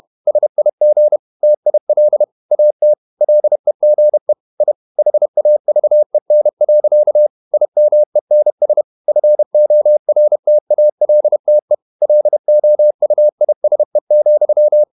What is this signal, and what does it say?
Morse code, heard as a keyed tone, about 31 wpm, 600 Hz. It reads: SIG TIL AT LEGE I HAVEN. IMENS FORTALTE LOUISE,
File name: Ferie QRQ 31wpm